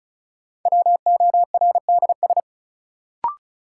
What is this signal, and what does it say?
Morse code, heard as a keyed tone, about 34 wpm, 700 Hz. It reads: WORDS
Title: Sentences from Top 1000 Words - Review Part 6 - New Word Drill - 35wpm